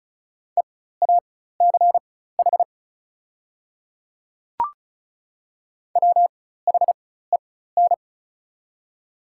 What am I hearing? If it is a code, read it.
Morse code, heard as a keyed tone, about 36 wpm, spaced as for 16 wpm, 690 Hz. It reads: EACH WHEN
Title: 4x Character Spacing - Top 100 Words - 35wpm